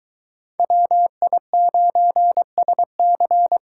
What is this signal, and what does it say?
Morse code, heard as a keyed tone, about 21 wpm, 700 Hz. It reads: WI9SC